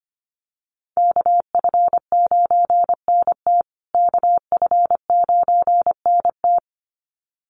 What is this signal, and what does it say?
Morse code, heard as a keyed tone, about 25 wpm, 700 Hz. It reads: KF9NT KF9NT
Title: USA Call Signs Repeated 2x 25wpm